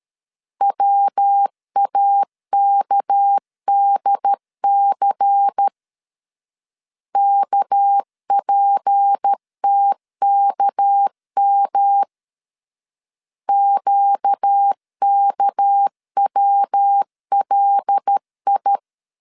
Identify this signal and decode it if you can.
Morse code, heard as a keyed tone, about 13 wpm, 800 Hz. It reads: WAKDC KPTKM QKWLI
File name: LEZIONI / 6 PKX0 / 6 PK 12wpm 2006